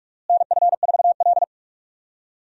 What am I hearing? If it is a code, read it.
Morse code, heard as a keyed tone, about 46 wpm, 700 Hz. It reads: NF4L
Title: USA Call Signs - 4 Characters - Repeated 3x - 45wpm